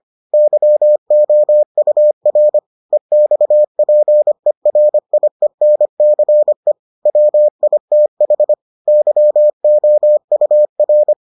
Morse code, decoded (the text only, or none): YOUR EXPERIENCE WITH YOUR